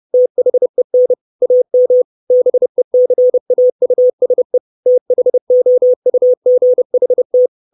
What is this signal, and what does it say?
Morse code, heard as a keyed tone, about 30 wpm, 500 Hz. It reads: THEN AM BECAUSE THOUGHT
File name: 632 Words 30WPM